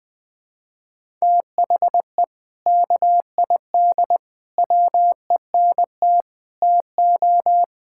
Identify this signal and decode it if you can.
Morse code, heard as a keyed tone, about 20 wpm, 700 Hz. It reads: THE KID WENT TO
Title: Sentences from Top 1600 Words 20wpm